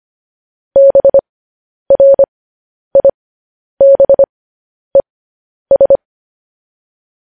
Morse code, heard as a keyed tone, about 25 wpm, spaced as for 10 wpm, 550 Hz. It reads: BRIBES